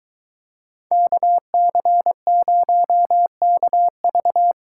Morse code, heard as a keyed tone, about 22 wpm, 700 Hz. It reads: KC0KV